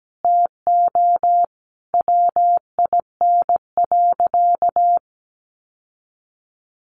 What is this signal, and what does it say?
Morse code, heard as a keyed tone, about 17 wpm, 700 Hz. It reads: TO WIN.
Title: Sentences from Top 1500 Words 17wpm